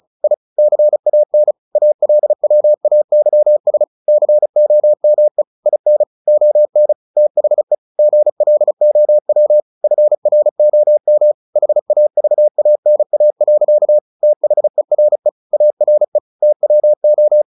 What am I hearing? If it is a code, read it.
Morse code, heard as a keyed tone, about 34 wpm, 600 Hz. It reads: I CAN ALWAYS COME IN ON THE GLOW FROM HAVANA. THERE ARE TWO
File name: the old man and the sea / the old man and the sea 1 35wpm